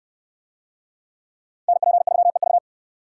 Morse code, heard as a keyed tone, about 67 wpm, 700 Hz. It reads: NP3EV